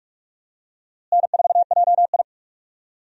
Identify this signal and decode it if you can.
Morse code, heard as a keyed tone, about 46 wpm, 700 Hz. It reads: N4JI
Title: USA Call Signs - 4 Characters - Repeated 2x - 45wpm